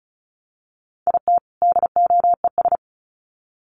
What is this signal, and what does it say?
Morse code, heard as a keyed tone, about 34 wpm, 700 Hz. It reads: IT DOES
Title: Sets of 2 Words 35wpm